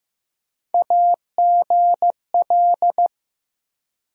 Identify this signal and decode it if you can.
Morse code, heard as a keyed tone, about 15 wpm, 700 Hz. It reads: AGL
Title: USA Call Sign Postfixes - 3 Letters 15wpm